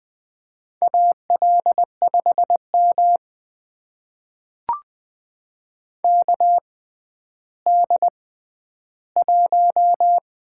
Morse code, heard as a keyed tone, about 20 wpm, spaced as for 11 wpm, 700 Hz. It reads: AL5M KD1
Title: USA Call Signs - 4 Characters - 6x Character Spacing - 20wpm